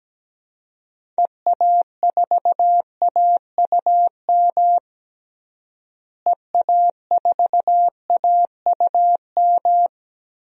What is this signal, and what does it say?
Morse code, heard as a keyed tone, about 17 wpm, 700 Hz. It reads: EA4AUM EA4AUM
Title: DXCC-100 Call Signs - 6 Characters - Repeated 3x - 17wpm